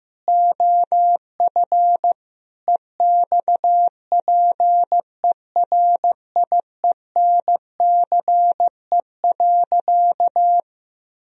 Morse code, heard as a keyed tone, about 15 wpm, 700 Hz. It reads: OF EXPERIENCE.
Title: Sentences from Top 1000 Words - Review Part 4 - 15wpm